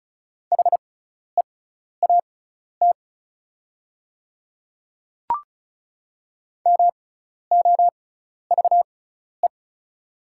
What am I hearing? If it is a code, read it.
Morse code, heard as a keyed tone, about 34 wpm, spaced as for 12 wpm, 690 Hz. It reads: HEAT MOVE
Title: 6x Character Spacing - Top 500 Words - 35wpm